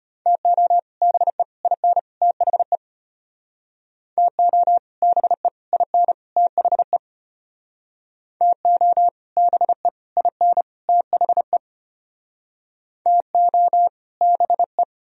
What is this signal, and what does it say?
Morse code, heard as a keyed tone, about 34 wpm, 700 Hz. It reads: TO BE IN THE TO BE IN THE TO BE IN THE TO BE